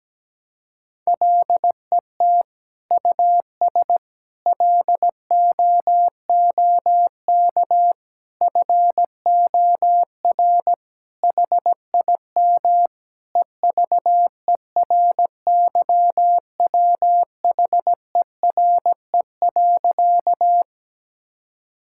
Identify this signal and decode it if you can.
Morse code, heard as a keyed tone, about 17 wpm, 700 Hz. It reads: LET US LOOK FOR HIM EVERYWHERE.
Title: Sentences from Top 1600 Words 17wpm